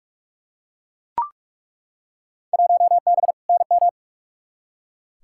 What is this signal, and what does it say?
Morse code, heard as a keyed tone, about 42 wpm, 700 Hz. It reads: 1B NM
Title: Field Day Exchanges Encore - 45wpm